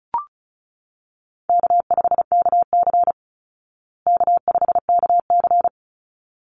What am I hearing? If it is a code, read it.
Morse code, heard as a keyed tone, about 34 wpm, 700 Hz. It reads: K5KC K5KC
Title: Easy Call Signs Repeated 2x - 35wpm